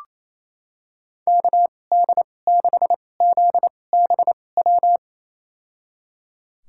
Morse code, heard as a keyed tone, about 28 wpm, spaced as for 20 wpm, 700 Hz. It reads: KD6ZBW
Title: USA Call Signs - 6 Characters - 2x Character Spacing - 28wpm